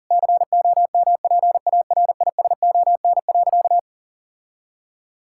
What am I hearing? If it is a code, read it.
Morse code, heard as a keyed tone, about 40 wpm, 700 Hz. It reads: COMPARISON.
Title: Sentences from Top 1600 Words 40wpm